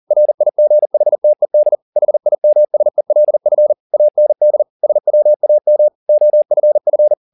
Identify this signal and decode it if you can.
Morse code, heard as a keyed tone, about 40 wpm, 600 Hz. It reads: RIGHTED HIMSELF AND SWAM OFF
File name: the old man and the sea 2 40wpm